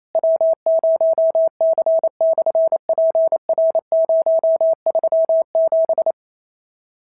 Morse code, heard as a keyed tone, about 28 wpm, 690 Hz. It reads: W0C/PR037
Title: SOTA QSO Practice 28wpm